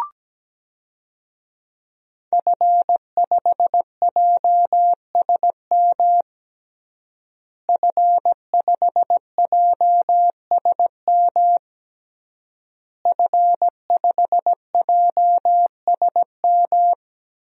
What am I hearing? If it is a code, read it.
Morse code, heard as a keyed tone, about 17 wpm, 700 Hz. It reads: F5JSM F5JSM F5JSM
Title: DXCC-100 Call Signs - 5 Characters - Repeated 3x - 17wpm